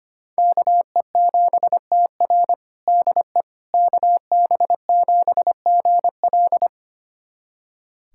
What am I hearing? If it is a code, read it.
Morse code, heard as a keyed tone, about 25 wpm, 700 Hz. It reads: KE7TR DE KB7GL
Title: USA Callsign DE Callsign - 5 Characters - 25wpm